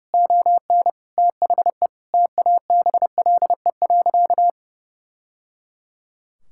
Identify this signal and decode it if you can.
Morse code, heard as a keyed tone, about 30 wpm, 700 Hz. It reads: ON THE TABLE.
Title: Sentences from Top 1000 Words - Review Part 5 - 30wpm